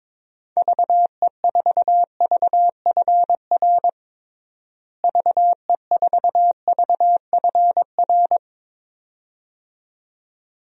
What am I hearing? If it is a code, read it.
Morse code, heard as a keyed tone, about 22 wpm, 700 Hz. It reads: VE4VFR VE4VFR
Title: DXCC-100 Call Signs - 6 Characters - Repeated 2x - 22wpm